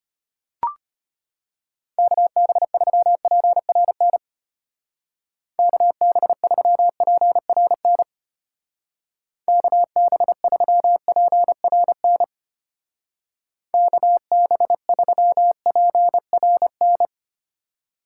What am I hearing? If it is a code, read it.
Morse code, heard as a keyed tone, about 32 wpm, 700 Hz. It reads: KB3PRN KB3PRN KB3PRN KB3PRN